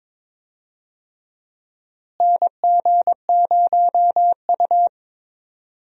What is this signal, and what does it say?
Morse code, heard as a keyed tone, about 22 wpm, 700 Hz. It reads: NG0U
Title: USA Call Signs - 4 Characters - Repeated 3x - 22wpm